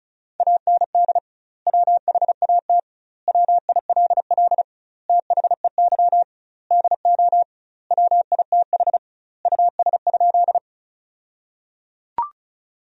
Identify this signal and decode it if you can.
Morse code, heard as a keyed tone, about 34 wpm, 700 Hz. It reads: AND WHAT WILL THEY DO WITH US?
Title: 2x Word-Spacing - Sentences from Top 200 Words - 35wpm